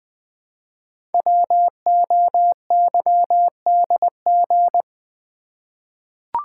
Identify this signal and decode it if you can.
Morse code, heard as a keyed tone, about 21 wpm, 700 Hz. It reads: WOYDG
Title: Code Groups - 5 Letters - 20wpm